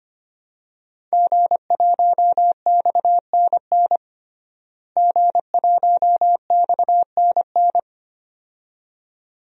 Morse code, heard as a keyed tone, about 25 wpm, 700 Hz. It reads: G1XNN G1XNN
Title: DXCC-100 Call Signs - 5 Characters - Repeated 2x - 25wpm